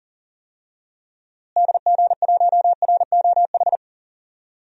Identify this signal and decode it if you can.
Morse code, heard as a keyed tone, about 40 wpm, 700 Hz. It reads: DG1ROH
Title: DXCC-100 Call Signs - 6 Characters - Repeated 2x - 40wpm